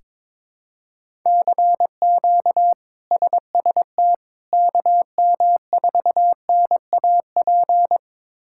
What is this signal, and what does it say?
Morse code, heard as a keyed tone, about 22 wpm, 700 Hz. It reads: CQ SST KM4NAP
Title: K1USN SST QSO Practice - 22wpm